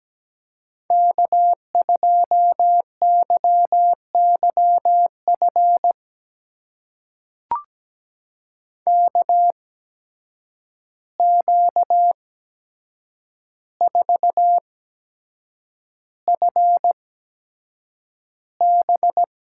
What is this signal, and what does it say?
Morse code, heard as a keyed tone, about 17 wpm, spaced as for 7 wpm, 700 Hz. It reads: K2YYF KQ4FB
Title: USA Call Signs - 5 Characters - 8x Character Spacing - 17wpm